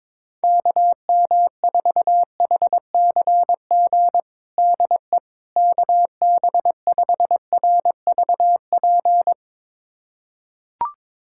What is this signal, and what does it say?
Morse code, heard as a keyed tone, about 22 wpm, 700 Hz. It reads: KM4HCG DE KB5RVP
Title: USA Callsign DE Callsign 22wpm